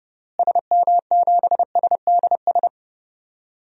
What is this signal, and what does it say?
Morse code, heard as a keyed tone, about 30 wpm, 700 Hz. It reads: SM7SDS
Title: DXCC-100 Call Signs - 6 Characters - Repeated 3x - 30wpm